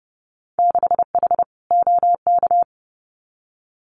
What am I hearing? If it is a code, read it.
Morse code, heard as a keyed tone, about 30 wpm, 700 Hz. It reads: 6H OK
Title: Winter Field Day Exchanges Encore 30wpm